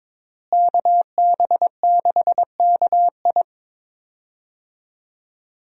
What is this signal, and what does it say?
Morse code, heard as a keyed tone, about 22 wpm, 690 Hz. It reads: KB6KI
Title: USA Call Signs - 5 Characters - Repeated 3x - 22wpm